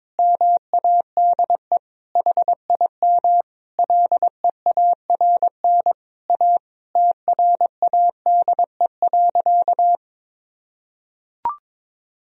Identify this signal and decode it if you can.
Morse code, heard as a keyed tone, about 22 wpm, 700 Hz. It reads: MADE HIM LEARN A TRADE.
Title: Sentences from Top 1000 Words - Review Part 4 - 22wpm